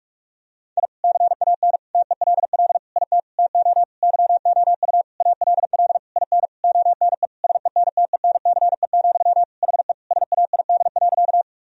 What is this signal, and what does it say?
Morse code, heard as a keyed tone, about 46 wpm, 700 Hz. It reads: I CAN TELL IT TO YOU ALL IN ONE SENTENCE, HE SAID.
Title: Sentences from Top 1000 Words - Review Part 1 - 45wpm